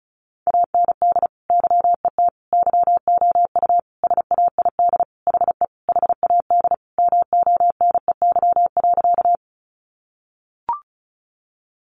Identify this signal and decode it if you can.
Morse code, heard as a keyed tone, about 34 wpm, 700 Hz. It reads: AND YET YOU SAID HE HAD MONEY.
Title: Sentences from Top 500 Words 35wpm